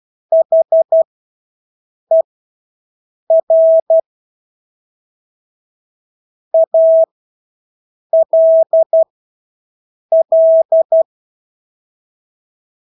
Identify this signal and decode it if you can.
Morse code, heard as a keyed tone, about 12 wpm, spaced as for 6 wpm, 650 Hz. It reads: HER ALL
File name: Engelske ord 12-6 wpm